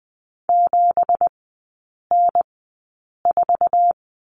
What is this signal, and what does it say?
Morse code, heard as a keyed tone, about 19 wpm, spaced as for 8 wpm, 700 Hz. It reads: 7N4
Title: 166 Lesson 27e - Number 7 - Three Characters 20wpm